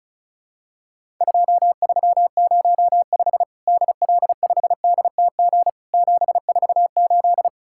Morse code, heard as a keyed tone, about 36 wpm, 700 Hz. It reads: 2305 DL5DTG 748
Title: Worked All Europe DX Contest - QTC Exchange - 35wpm